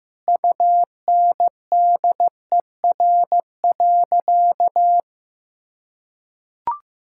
Morse code, heard as a keyed tone, about 16 wpm, 700 Hz. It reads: UNDER.
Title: Sentences from Top 1600 Words 15wpm